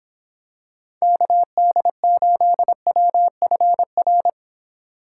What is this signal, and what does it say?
Morse code, heard as a keyed tone, about 25 wpm, 700 Hz. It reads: KD8WFR